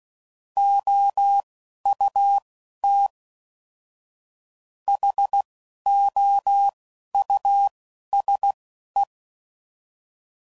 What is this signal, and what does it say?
Morse code, heard as a keyed tone, about 16 wpm, spaced as for 11 wpm, 800 Hz. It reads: OUT HOUSE
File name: Suffixes / General / Houses